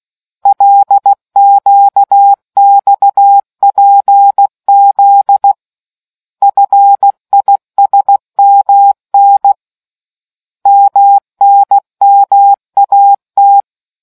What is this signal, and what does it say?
Morse code, heard as a keyed tone, about 16 wpm, 800 Hz. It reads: LQXPZ FISMN MNMAT